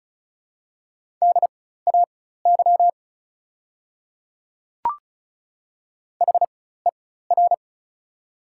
Morse code, heard as a keyed tone, about 34 wpm, spaced as for 16 wpm, 690 Hz. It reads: DAY HER
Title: 4x Character Spacing - Top 100 Words - 35wpm